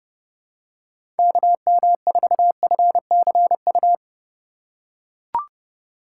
Morse code, heard as a keyed tone, about 30 wpm, 700 Hz. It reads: KM4FCU
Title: USA Call Signs - 6 Characters - 2x Character Spacing - 30wpm